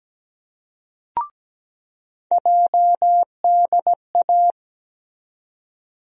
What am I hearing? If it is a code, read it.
Morse code, heard as a keyed tone, about 17 wpm, 700 Hz. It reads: JDA
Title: USA Call Sign Postfixes - 3 Letters 17wpm